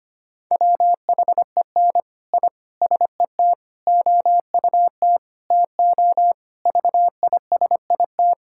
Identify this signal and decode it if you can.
Morse code, heard as a keyed tone, about 25 wpm, 700 Hz. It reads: WHEN I SET OUT TO VISIT
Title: Sentences from Top 1000 Words - Review Part 6 - 25wpm